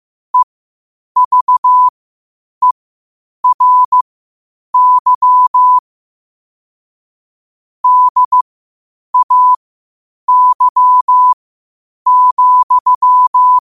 Morse code, heard as a keyed tone, about 15 wpm, spaced as for 8 wpm, 1000 Hz. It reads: EVERY DAY,
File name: qotdfiles / 8wpm